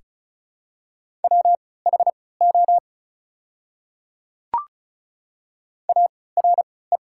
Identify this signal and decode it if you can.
Morse code, heard as a keyed tone, about 33 wpm, spaced as for 19 wpm, 690 Hz. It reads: WHO ARE
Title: 3x Character Spacing - Top 100 Words - 35wpm